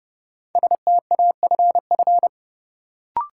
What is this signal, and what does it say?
Morse code, heard as a keyed tone, about 30 wpm, 700 Hz. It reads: STAFF
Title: Sentences from Top 1200 Words - New Word Drill - 30wpm